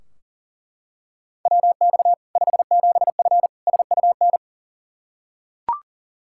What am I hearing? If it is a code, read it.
Morse code, heard as a keyed tone, about 40 wpm, 700 Hz. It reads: WX 57F SUN
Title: K1USN SST QSO Practice 40wpm